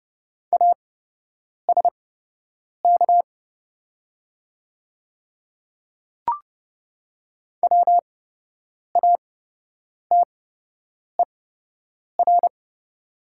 Morse code, heard as a keyed tone, about 30 wpm, spaced as for 8 wpm, 700 Hz. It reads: ASK WATER
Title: 8x Character Spacing - Top 200 Words - 30wpm